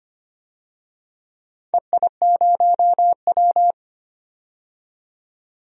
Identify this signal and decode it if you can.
Morse code, heard as a keyed tone, about 25 wpm, 700 Hz. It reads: EI0W
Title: DXCC-100 Call Signs - 4 Characters - 25wpm